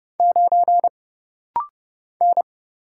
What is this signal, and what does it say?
Morse code, heard as a keyed tone, about 31 wpm, 700 Hz. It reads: 9 N
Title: Single Letter-Number Alt - Rapid-Fire 30wpm